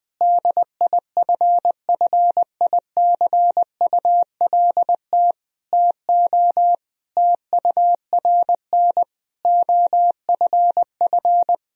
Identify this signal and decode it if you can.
Morse code, heard as a keyed tone, about 21 wpm, 700 Hz. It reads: DIFFICULT TO TURN OFF
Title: Sentences from Top 1000 Words - Review Part 3 - 20wpm